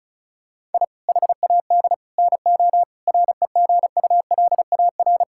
Morse code, heard as a keyed tone, about 36 wpm, 690 Hz. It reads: I HAD NO REGULAR
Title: Sentences from Top 1000 Words - Review Part 6 - 35wpm